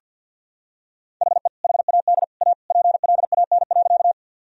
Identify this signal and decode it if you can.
Morse code, heard as a keyed tone, about 49 wpm, 700 Hz. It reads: HE HAD A PLAN.
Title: Sentences from Top 1000 Words - Review Part 2 - 50wpm